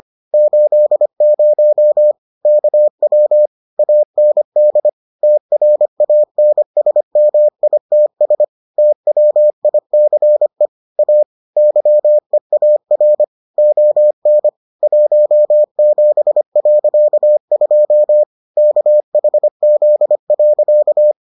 Morse code, heard as a keyed tone, about 25 wpm, 600 Hz. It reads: 80 KW AND TRANSMITS TWICE A YEAR ON 17.2 KHZ.